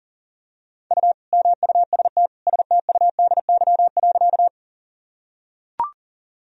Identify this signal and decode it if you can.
Morse code, heard as a keyed tone, about 40 wpm, 700 Hz. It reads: U MUST STUDY.
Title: Sentences from Top 300 Words and Abbreviations 40wpm